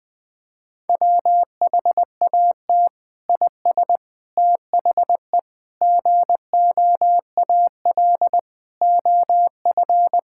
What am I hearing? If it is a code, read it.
Morse code, heard as a keyed tone, about 20 wpm, 700 Hz. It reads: WHAT IS THE GOAL OF
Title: Sentences from Top 1000 Words - Review Part 5 - 20wpm